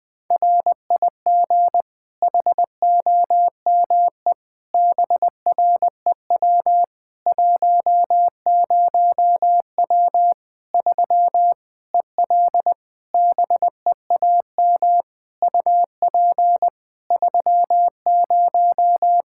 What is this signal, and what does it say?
Morse code, heard as a keyed tone, about 21 wpm, 700 Hz. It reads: RIG HOME BREW 10W 3 EL BEAM UP 30
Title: Practice QSO - Exchange 2 - Info - 20wpm